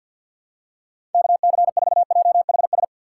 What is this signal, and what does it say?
Morse code, heard as a keyed tone, about 54 wpm, 700 Hz. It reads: KC4JHS
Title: USA Call Signs - 6 Characters - 6x Character Spacing - 50wpm